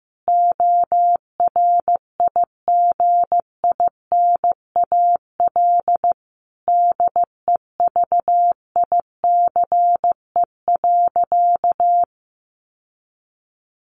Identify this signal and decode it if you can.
Morse code, heard as a keyed tone, about 15 wpm, 700 Hz. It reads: ORIGINAL DEVICE.
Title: Sentences from Top 1500 Words 15wpm